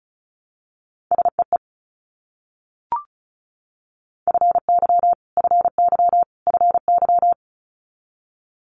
Morse code, heard as a keyed tone, about 36 wpm, 700 Hz. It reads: SEE FY FY FY
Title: Multi-letter Phonemes Repeated 3x 35wpm